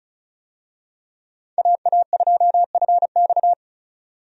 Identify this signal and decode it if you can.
Morse code, heard as a keyed tone, about 36 wpm, 700 Hz. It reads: AA2FX
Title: USA Call Signs - 5 Characters - 35wpm